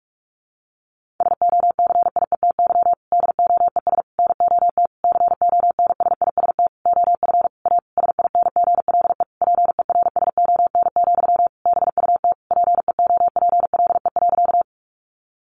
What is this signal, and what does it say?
Morse code, heard as a keyed tone, about 46 wpm, 700 Hz. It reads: SOCIETY DOES NOT CONSIST OF A SINGLE PERSON, BUT PEOPLE.
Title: Sentences from Top 1400 Words 45wpm